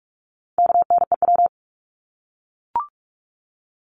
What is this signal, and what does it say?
Morse code, heard as a keyed tone, about 44 wpm, 700 Hz. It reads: KNEW
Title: Top 500 Words Repeated 45wpm